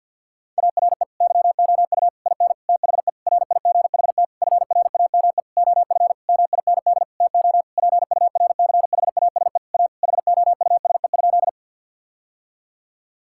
Morse code, heard as a keyed tone, about 50 wpm, 700 Hz. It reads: ARE YOU IN THE RIGHT FRAME OF MIND TO PURCHASE A HOUSE?
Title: Sentences from Top 1000 Words - Review Part 6 - 50wpm